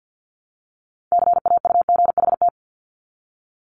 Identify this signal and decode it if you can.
Morse code, heard as a keyed tone, about 50 wpm, 700 Hz. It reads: CAUGHT
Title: Multi-letter Phonemes Repeated 3x 50wpm